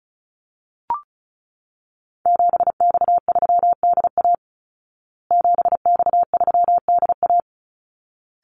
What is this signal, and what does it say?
Morse code, heard as a keyed tone, about 34 wpm, 700 Hz. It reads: 7X3DA 7X3DA
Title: Call Signs Repeated 3x - 35wpm